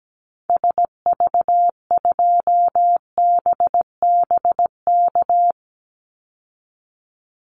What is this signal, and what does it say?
Morse code, heard as a keyed tone, about 17 wpm, 700 Hz. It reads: SV2BBK
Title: Call Signs Repeated 2x - 17wpm